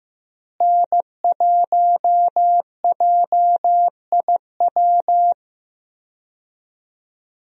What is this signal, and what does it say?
Morse code, heard as a keyed tone, about 15 wpm, 700 Hz. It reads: N1JIW
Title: USA Call Signs - 5 Characters - Repeated 3x - 15wpm